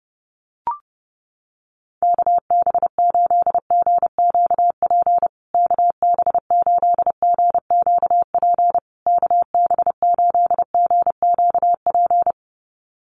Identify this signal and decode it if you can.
Morse code, heard as a keyed tone, about 30 wpm, 700 Hz. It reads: KB8GQP KB8GQP KB8GQP
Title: USA Call Signs Repeated 3x 30wpm